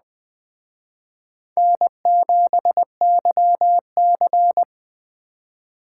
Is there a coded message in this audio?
Yes: Morse code from 1.6 to 4.6 s, heard as a keyed tone, about 19 wpm, 700 Hz. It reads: N7YC